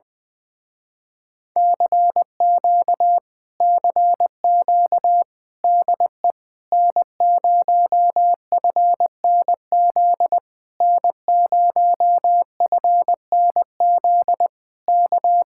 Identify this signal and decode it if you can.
Morse code, heard as a keyed tone, about 21 wpm, 700 Hz. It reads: CQ CQ DE N0FNZ N0FNZ K
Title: Speed-Racing Common QSO Phrases - 20wpm